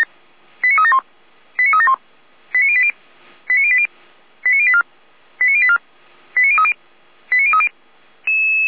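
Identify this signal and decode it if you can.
Five-tone selective call (ZVEI), heard as quick runs of five tones, tones of about 70 ms. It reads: 89371 89371 89080 89080 89084 89084 89030 89030
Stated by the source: ZVEI five-tone selective call received by SDR.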